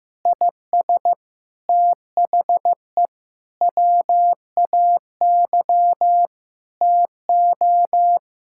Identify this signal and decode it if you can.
Morse code, heard as a keyed tone, about 15 wpm, 700 Hz. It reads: IS THE WAY TO
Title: Sentences from Top 1000 Words - Review Part 5 - 15wpm